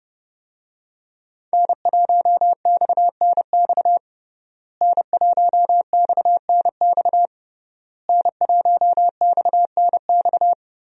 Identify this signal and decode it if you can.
Morse code, heard as a keyed tone, about 30 wpm, 700 Hz. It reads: N1XNX N1XNX N1XNX
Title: USA Call Signs - 5 Characters - Repeated 3x - 30wpm